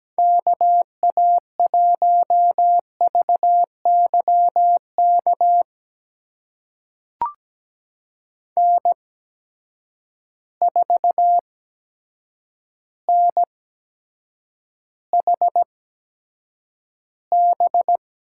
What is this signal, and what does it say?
Morse code, heard as a keyed tone, about 17 wpm, spaced as for 6 wpm, 700 Hz. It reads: KA1VY KN4NHB
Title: USA Call Signs - 8x Character Spacing - 17wpm